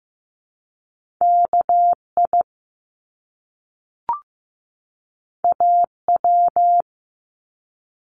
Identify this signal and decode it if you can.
Morse code, heard as a keyed tone, about 15 wpm, 700 Hz. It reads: KI AW
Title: Top Two Letter Combinations in Words 15wpm